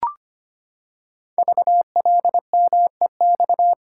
Morse code, heard as a keyed tone, about 25 wpm, 700 Hz. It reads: VLMEX
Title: Code Groups - 5 Letters - 25wpm